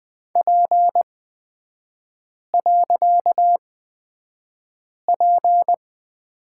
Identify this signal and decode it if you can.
Morse code, heard as a keyed tone, about 20 wpm, spaced as for 5 wpm, 700 Hz. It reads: P.P
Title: 255 Lesson 41e - Period - Three Characters 20wpm